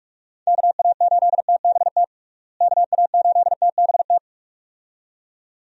Morse code, heard as a keyed tone, about 46 wpm, 700 Hz. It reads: KA8TBT KA8TBT
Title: USA Call Signs - 6 Characters - Repeated 2x - 45wpm